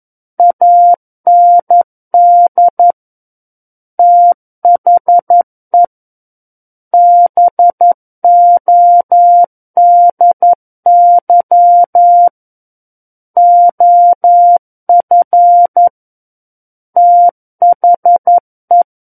Morse code, heard as a keyed tone, about 11 wpm, 700 Hz. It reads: AND THE BODY OF THE
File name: Book One / 02-Chapter 2 - 11 WPM